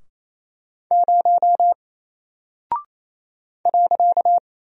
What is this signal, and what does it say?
Morse code, heard as a keyed tone, about 28 wpm, 700 Hz. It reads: 0 .
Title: Single Character (Letter:Number:Punctuation:Prosign) Alt 28wpm